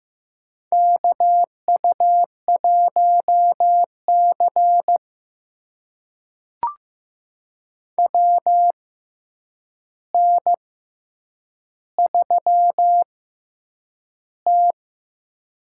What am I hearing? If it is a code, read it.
Morse code, heard as a keyed tone, about 15 wpm, spaced as for 8 wpm, 700 Hz. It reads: KU1C WN3T
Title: USA Call Signs - 4 Characters - 6x Character Spacing - 15wpm